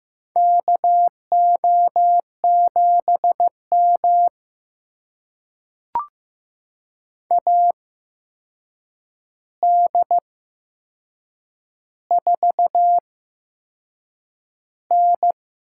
Morse code, heard as a keyed tone, about 15 wpm, spaced as for 6 wpm, 700 Hz. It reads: KO7M AD4N
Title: USA Call Signs - 4 Characters - 8x Character Spacing - 15wpm